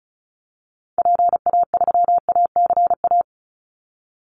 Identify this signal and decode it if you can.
Morse code, heard as a keyed tone, about 34 wpm, 700 Hz. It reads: PA3ACA
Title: Call Signs - 35wpm